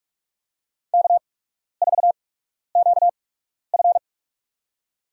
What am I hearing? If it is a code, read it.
Morse code, heard as a keyed tone, about 46 wpm, spaced as for 12 wpm, 700 Hz. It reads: K4QF
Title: USA Call Signs - 8x Character Spacing - 45wpm